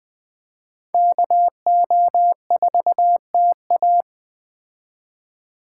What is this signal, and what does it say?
Morse code, heard as a keyed tone, about 20 wpm, 700 Hz. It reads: KO4TA
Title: Speed Racing - USA Call Signs 17wpm